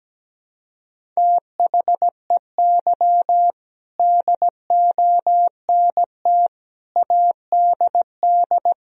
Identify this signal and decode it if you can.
Morse code, heard as a keyed tone, about 17 wpm, 700 Hz. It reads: THEY DONT ADD
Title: Sentences from Top 1000 Words - Review Part 1 - 17wpm